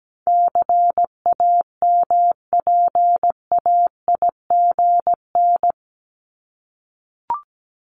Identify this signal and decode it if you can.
Morse code, heard as a keyed tone, about 17 wpm, 700 Hz. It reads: CAMPAIGN
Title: Top 1001-1250 Words 17wpm